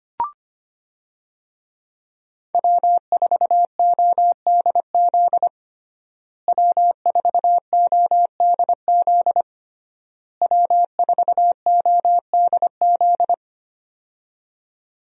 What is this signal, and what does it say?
Morse code, heard as a keyed tone, about 25 wpm, 700 Hz. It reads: W4ODZ W4ODZ W4ODZ
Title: USA Call Signs - 5 Characters - Repeated 3x - 25wpm